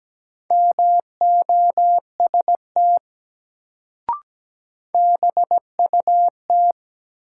Morse code, heard as a keyed tone, about 17 wpm, 700 Hz. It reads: MOST BUT
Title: Sets of 2 Words - New Word Drill - 17wpm